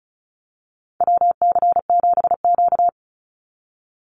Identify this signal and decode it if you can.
Morse code, heard as a keyed tone, about 36 wpm, 700 Hz. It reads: WC7Q
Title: Call Signs - 35wpm